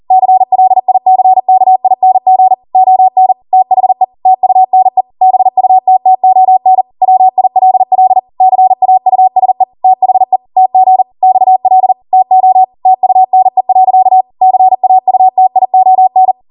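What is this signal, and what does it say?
Morse code, heard as a keyed tone, about 40 wpm, 750 Hz. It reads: CLICKING ON THE TUNE BUTTON WILL CAUSE THE TG XL TO TUNE. CAUTION
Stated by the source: Code practice at 40 wpm.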